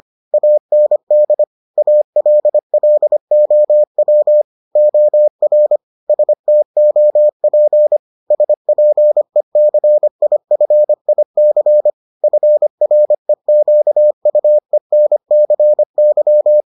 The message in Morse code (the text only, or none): AND ALLOW OR STOP SPECIFIC FREQUENCY